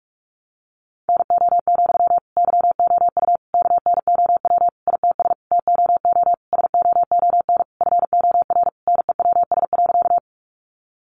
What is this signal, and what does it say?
Morse code, heard as a keyed tone, about 44 wpm, 700 Hz. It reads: NO, YOU KNOW ITS TOO SOON FOR NEWS.
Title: Sentences from Top 800 Words 45wpm